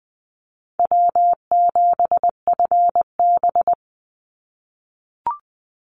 Morse code, heard as a keyed tone, about 20 wpm, 700 Hz. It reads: W7FB
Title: Call Signs - 20wpm